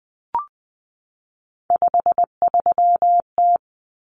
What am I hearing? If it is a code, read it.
Morse code, heard as a keyed tone, about 20 wpm, 700 Hz. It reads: 53T
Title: Common Cut Numbers - 3-Digit 20wpm